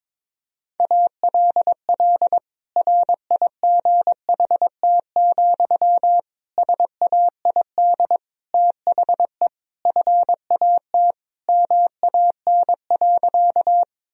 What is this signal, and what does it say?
Morse code, heard as a keyed tone, about 22 wpm, 700 Hz. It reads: ALL RIGHT, SAID THE FAT MAN.
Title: Sentences from Top 1000 Words - Review Part 4 - 22wpm